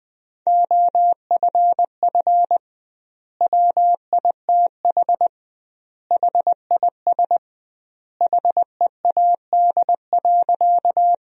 Morse code, heard as a keyed tone, about 20 wpm, 700 Hz. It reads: OFF WITH HIS HEAD.
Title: 2x Word-Spacing - Sentences from Top 300 Words - 20wpm